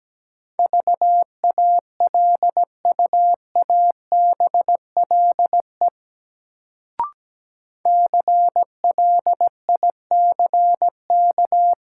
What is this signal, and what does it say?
Morse code, heard as a keyed tone, about 17 wpm, 700 Hz. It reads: VALUABLE CLICK
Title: Sentences from Top 1100 Words - New Word Drill - 17wpm